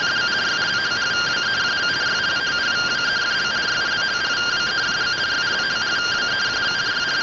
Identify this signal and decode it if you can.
Two-tone teleprinter signal, about 170 Hz shift.
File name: findings / CT SAGITAL 27.gen-40 rtty rx